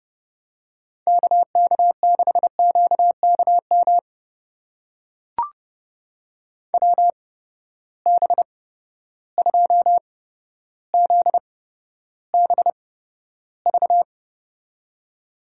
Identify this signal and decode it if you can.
Morse code, heard as a keyed tone, about 32 wpm, spaced as for 13 wpm, 700 Hz. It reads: KK6QKM WB2ZBV
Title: USA Call Signs - 8x Character Spacing - 30wpm